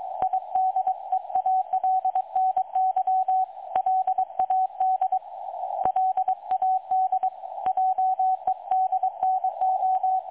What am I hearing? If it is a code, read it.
Morse code, heard as a keyed tone, about 22 wpm, 750 Hz, with radio noise. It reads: IDEALNY LAD LAD JEDNO
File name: dlazaawansowanych / nagraniazpasma / websdr recording 3565.0kHz